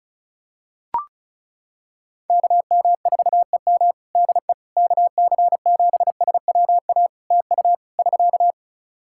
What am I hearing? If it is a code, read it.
Morse code, heard as a keyed tone, about 34 wpm, 700 Hz. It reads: KM4EM DE KC7SWA TU <SK>
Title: Practice QSO - Exchange 4 - Ending - 35wpm